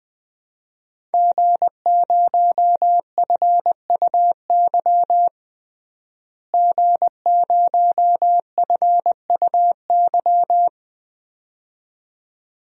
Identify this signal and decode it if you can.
Morse code, heard as a keyed tone, about 21 wpm, 700 Hz. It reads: G0FUY G0FUY
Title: DXCC-100 Call Signs - 5 Characters - Repeated 2x - 20wpm